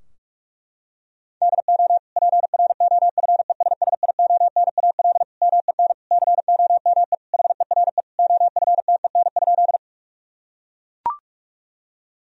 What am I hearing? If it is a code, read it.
Morse code, heard as a keyed tone, about 41 wpm, 700 Hz. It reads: DO PROFESSIONAL MEN COME HERE OFTEN?
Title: Sentences from Top 1000 Words - Review Part 4 - 45wpm